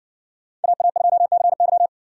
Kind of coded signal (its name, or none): Morse code